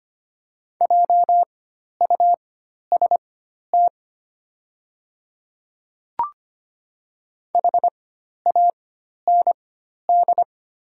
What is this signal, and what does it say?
Morse code, heard as a keyed tone, about 25 wpm, spaced as for 12 wpm, 700 Hz. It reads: JUST HAND
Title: 4x Character Spacing - Top 200 Words - 25wpm